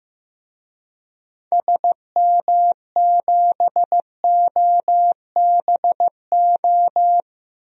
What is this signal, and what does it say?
Morse code, heard as a keyed tone, about 15 wpm, 700 Hz. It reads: SM7OBO
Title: DXCC-100 Call Signs - 6 Characters - Repeated 3x - 15wpm